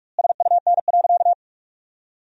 Morse code, heard as a keyed tone, about 46 wpm, 700 Hz. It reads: SUN.
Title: 2x Word-Spacing - Sentences from Top 500 Words - 45wpm